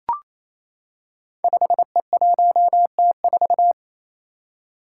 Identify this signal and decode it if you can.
Morse code, heard as a keyed tone, about 28 wpm, 700 Hz. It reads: 5E1T4
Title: Five Challenging Letter-Number Combinations 28wpm